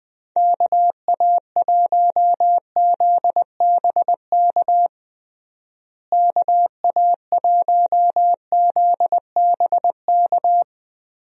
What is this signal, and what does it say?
Morse code, heard as a keyed tone, about 21 wpm, 700 Hz. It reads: KA1ZBK KA1ZBK
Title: USA Call Signs - 6 Characters - Repeated 3x - 20wpm